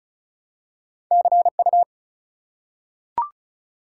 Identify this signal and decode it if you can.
Morse code, heard as a keyed tone, about 34 wpm, 700 Hz. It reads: CU
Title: Top 180 QSO Elements 35wpm